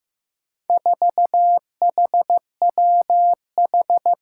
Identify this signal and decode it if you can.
Morse code, heard as a keyed tone, about 15 wpm, 700 Hz. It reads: 4HWH
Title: Four Challenging Letter-Number Combinations 15wpm